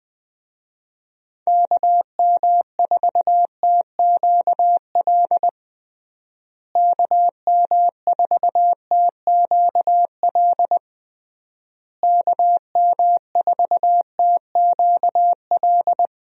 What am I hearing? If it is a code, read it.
Morse code, heard as a keyed tone, about 21 wpm, 700 Hz. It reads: KM4TQL KM4TQL KM4TQL
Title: USA Call Signs - 6 Characters - Repeated 3x - 20wpm